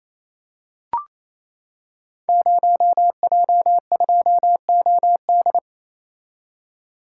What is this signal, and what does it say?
Morse code, heard as a keyed tone, about 27 wpm, 700 Hz. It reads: 0J2OD
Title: Code Groups - 5 Letter:Numbers - 28wpm